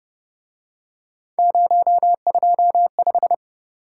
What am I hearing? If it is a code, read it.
Morse code, heard as a keyed tone, about 29 wpm, 700 Hz. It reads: 025
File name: Speed-Racing - Three Letter-Number Combinations - 25wpm